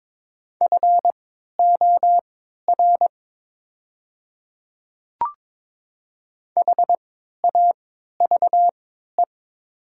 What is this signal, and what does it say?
Morse code, heard as a keyed tone, about 22 wpm, spaced as for 12 wpm, 700 Hz. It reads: FOR HAVE
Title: 3x Character Spacing - Top 200 Words - 22wpm